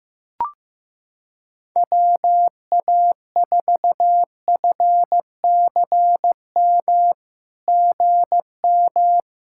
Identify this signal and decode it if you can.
Morse code, heard as a keyed tone, about 15 wpm, 700 Hz. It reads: WA4FCM GM
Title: SOTA QSO Practice 15wpm